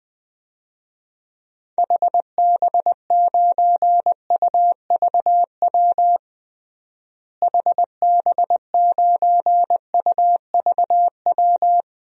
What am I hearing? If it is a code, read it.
Morse code, heard as a keyed tone, about 21 wpm, 700 Hz. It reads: HB9UVW HB9UVW
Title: DXCC-100 Call Signs - 6 Characters - Repeated 2x - 20wpm